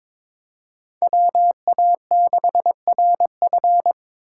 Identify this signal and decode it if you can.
Morse code, heard as a keyed tone, about 22 wpm, 700 Hz. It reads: WA6RF
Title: USA Call Signs - 4x Character Spacing - 22wpm